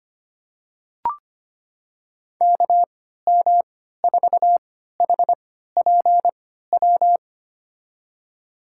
Morse code, heard as a keyed tone, about 25 wpm, spaced as for 14 wpm, 700 Hz. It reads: KM4HPW
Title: USA Call Signs - 6 Characters - 3x Character Spacing - 25wpm